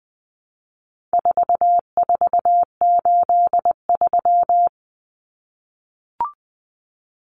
Morse code, heard as a keyed tone, about 20 wpm, 700 Hz. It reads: 4483
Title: Common Cut Numbers - 4-Digit 20wpm